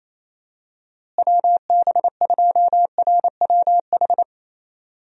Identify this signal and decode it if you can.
Morse code, heard as a keyed tone, about 28 wpm, 700 Hz. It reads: WB2RWH
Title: USA Call Signs - 6 Characters - Repeated 3x - 28wpm